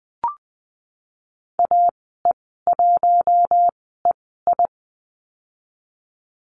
Morse code, heard as a keyed tone, about 20 wpm, spaced as for 15 wpm, 700 Hz. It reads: AE1EI
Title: 048 Lesson 9g - Number 4 - Call Signs 20wpm